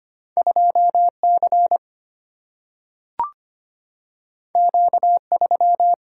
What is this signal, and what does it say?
Morse code, heard as a keyed tone, about 25 wpm, 700 Hz. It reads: 2C Q3
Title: Single-Letter Single-Number 25wpm